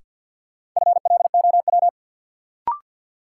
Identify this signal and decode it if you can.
Morse code, heard as a keyed tone, about 47 wpm, 700 Hz. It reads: FLOW
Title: Sentences from Top 1200 Words - New Word Drill Alt - 50wpm